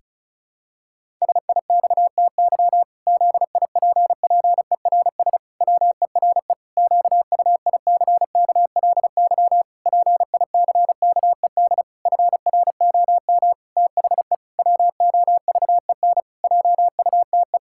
Morse code, heard as a keyed tone, about 36 wpm, 700 Hz. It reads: SIXTY ZIPPERS WERE QUICKLY PICKED FROM THE WOVEN JUTE
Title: Pangram Challenge v2 - 35wpm